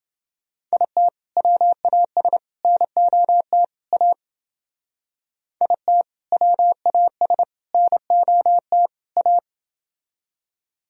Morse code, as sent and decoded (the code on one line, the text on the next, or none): .. - / .-- .- ... / -. --- - / .- / .. - / .-- .- ... / -. --- - / .-
IT WAS NOT A IT WAS NOT A